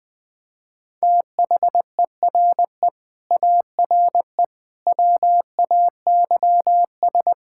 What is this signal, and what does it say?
Morse code, heard as a keyed tone, about 21 wpm, 700 Hz. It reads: THERE ARE WAYS
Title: Sentences with 8-Word Phrases - 20wpm